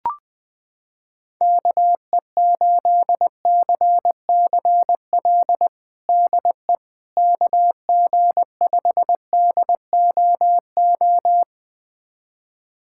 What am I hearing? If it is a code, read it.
Morse code, heard as a keyed tone, about 20 wpm, 700 Hz. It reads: KE8CCL DE KG5DOO
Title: USA Callsign DE Callsign - 6 Characters - 20wpm